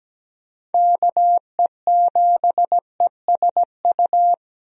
Morse code, heard as a keyed tone, about 17 wpm, 700 Hz. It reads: KE7ESU
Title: USA Call Signs - 6 Characters - 17wpm